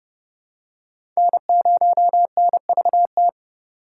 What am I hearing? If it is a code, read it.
Morse code, heard as a keyed tone, about 31 wpm, 700 Hz. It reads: N0NVT
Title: USA Call Signs - 5 Characters - 2x Character Spacing - 30wpm